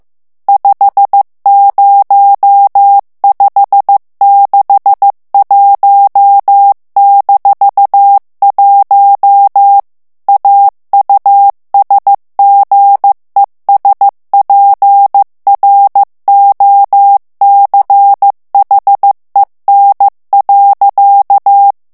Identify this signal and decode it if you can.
Morse code, heard as a keyed tone, about 15 wpm, 800 Hz. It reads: 50561-1 AUSGESPROCHEN.
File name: ALLEMAND texte clair 1 vitesse 14 mots minute